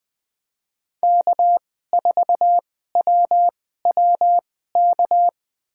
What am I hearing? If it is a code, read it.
Morse code, heard as a keyed tone, about 21 wpm, spaced as for 15 wpm, 700 Hz. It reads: K4WWK
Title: USA Call Signs - 5 Characters - 2x Character Spacing - 20wpm